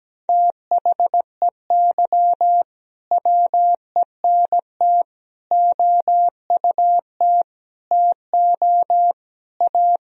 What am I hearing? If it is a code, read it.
Morse code, heard as a keyed tone, about 17 wpm, 700 Hz. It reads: THEY WENT OUT TO A
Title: Sentences from Top 1000 Words - Review Part 2 - 17wpm